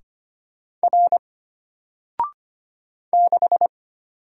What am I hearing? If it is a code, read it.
Morse code, heard as a keyed tone, about 25 wpm, 700 Hz. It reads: R 6
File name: Single-Letter-Number-25wpm